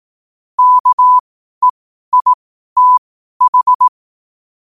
Morse code, heard as a keyed tone, about 18 wpm, spaced as for 12 wpm, 1000 Hz. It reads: KEITH
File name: qotdfiles / 12wpm